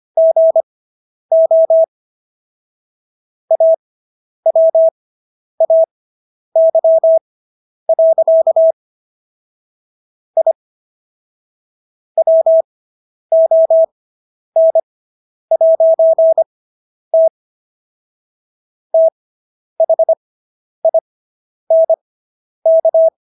Morse code, heard as a keyed tone, about 25 wpm, spaced as for 10 wpm, 650 Hz. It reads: GO AWAY. I WON'T THINK